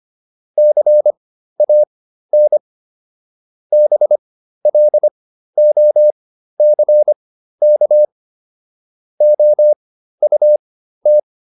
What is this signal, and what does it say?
Morse code, heard as a keyed tone, about 25 wpm, spaced as for 13 wpm, 600 Hz. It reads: CAN BLOCK OUT